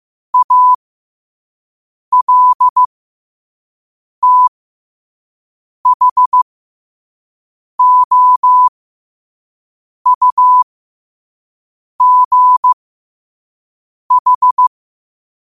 Morse code, heard as a keyed tone, about 15 wpm, spaced as for 5 wpm, 1000 Hz. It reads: ALTHOUGH
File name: qotdfiles / 5wpm